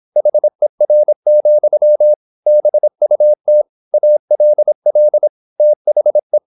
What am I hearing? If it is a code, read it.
Morse code, heard as a keyed tone, about 26 wpm, 600 Hz. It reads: HER, BUT ALL THE
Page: Never 26wpm | CW med Gnister